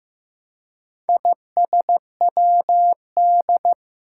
Morse code, heard as a keyed tone, about 15 wpm, 700 Hz. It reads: ISWD
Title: Four Challenging Letter-Number Combinations 15wpm